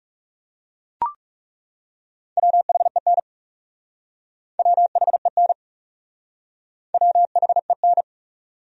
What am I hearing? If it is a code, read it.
Morse code, heard as a keyed tone, about 37 wpm, 700 Hz. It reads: WHEN WHEN WHEN
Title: Speed-Racing Top 100 Words 30wpm